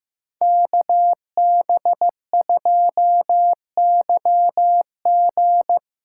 Morse code, heard as a keyed tone, about 15 wpm, 700 Hz. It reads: KB2YG
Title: USA Callsign DE Callsign - 5 Characters - 15wpm